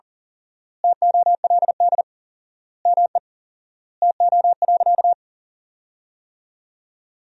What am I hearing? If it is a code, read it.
Morse code, heard as a keyed tone, about 40 wpm, 700 Hz. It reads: TOLD ME TO.
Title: 4x Word-Spacing - Sentences from Top 400 Words - 40wpm